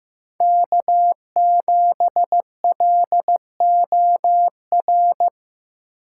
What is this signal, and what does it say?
Morse code, heard as a keyed tone, about 15 wpm, 700 Hz. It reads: K7LOR
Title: USA Call Signs - 5 Characters - 2x Character Spacing - 15wpm